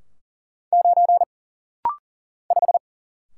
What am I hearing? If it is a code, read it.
Morse code, heard as a keyed tone, about 39 wpm, 700 Hz. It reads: 9 5
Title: Single Letter-Number - Rapid-Fire 40wpm